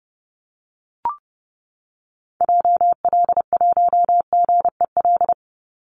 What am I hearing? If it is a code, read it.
Morse code, heard as a keyed tone, about 30 wpm, 700 Hz. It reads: JL1GEL
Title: Call Signs - 30wpm